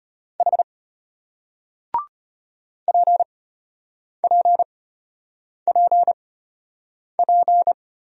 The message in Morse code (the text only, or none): H PPPP